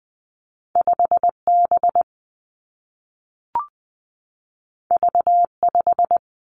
Morse code, heard as a keyed tone, about 20 wpm, 700 Hz. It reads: 5 B V 5
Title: Two Challenging Letter-Number Combinations 20wpm